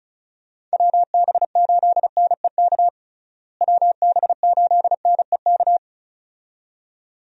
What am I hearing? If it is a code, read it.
Morse code, heard as a keyed tone, about 36 wpm, 700 Hz. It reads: WB8NEK WB8NEK
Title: USA Call Signs - 6 Characters - Repeated 2x - 35wpm